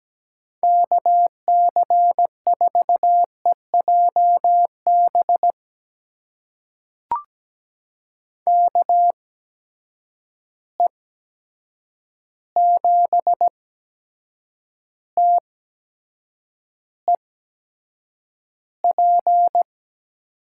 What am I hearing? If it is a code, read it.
Morse code, heard as a keyed tone, about 17 wpm, spaced as for 7 wpm, 700 Hz. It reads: KC4EJB KE7TEP
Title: USA Call Signs - 8x Character Spacing - 17wpm